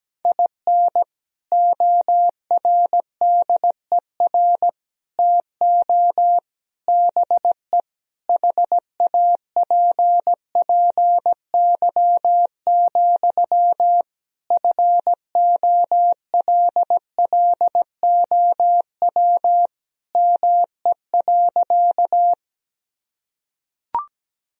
Morse code, heard as a keyed tone, about 17 wpm, 700 Hz. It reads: IN ORDER TO BE HAPPY, FOLLOW ME.
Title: Sentences from Top 1000 Words - Review Part 2 - 17wpm